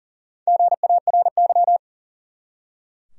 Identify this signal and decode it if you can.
Morse code, heard as a keyed tone, about 40 wpm, 700 Hz. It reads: GARY
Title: USA Common Names - Max 5 Letters - 40wpm